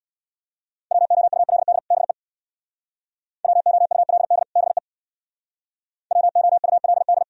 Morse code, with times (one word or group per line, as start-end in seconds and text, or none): 0.9-1.8 WOULD
1.9-2.1 BE
3.4-4.4 WOULD
4.6-4.8 BE
6.1-7.2 WOULD